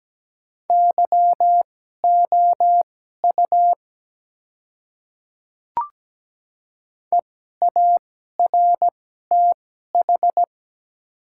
Morse code, heard as a keyed tone, about 17 wpm, spaced as for 12 wpm, 700 Hz. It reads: YOU EARTH
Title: 2x Character Spacing - Top 200 Words - 17wpm